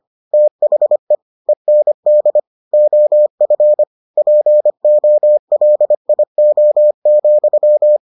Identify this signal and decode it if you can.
Morse code, heard as a keyed tone, about 25 wpm, 600 Hz. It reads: THE END OF POLIO,